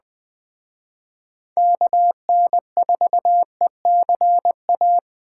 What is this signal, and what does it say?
Morse code, heard as a keyed tone, about 20 wpm, 700 Hz. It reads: KN4ECA